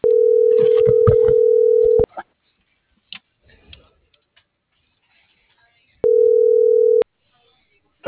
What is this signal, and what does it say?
Ringback tone (North America), 440 + 480 Hz.
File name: CENTRAL BLN / out ximena.diaz - .7308